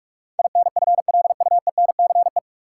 Morse code, heard as a keyed tone, about 41 wpm, 700 Hz. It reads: INFLUENCE
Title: Sentences from Top 1100 Words - New Word Drill - 45wpm